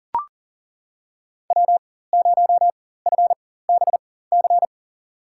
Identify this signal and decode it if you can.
Morse code, heard as a keyed tone, about 40 wpm, spaced as for 19 wpm, 700 Hz. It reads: W0FBC
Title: USA Call Signs - 5 Characters - 4x Character Spacing - 40wpm